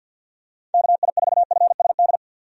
Morse code, heard as a keyed tone, about 50 wpm, 700 Hz. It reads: KI4FSD
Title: USA Call Signs - 6 Characters - 50wpm